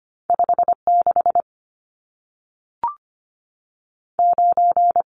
Morse code, heard as a keyed tone, about 25 wpm, spaced as for 3 wpm, 700 Hz. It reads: 5 69
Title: Numbers - 2-Digit 25wpm